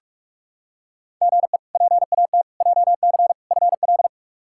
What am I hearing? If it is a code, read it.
Morse code, heard as a keyed tone, about 46 wpm, 700 Hz. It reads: GE PAT JC FL
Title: K1USN SST QSO Practice - 45wpm